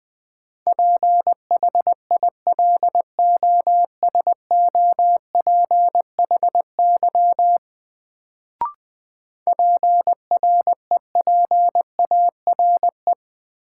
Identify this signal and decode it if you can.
Morse code, heard as a keyed tone, about 21 wpm, 700 Hz. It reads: PHILOSOPHY PREPARE
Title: Sentences from Top 900 Words - New Word Drill - 20wpm